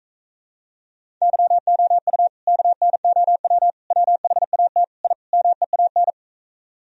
Morse code, heard as a keyed tone, about 44 wpm, 700 Hz. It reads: YOU KNOW WHAT I MEAN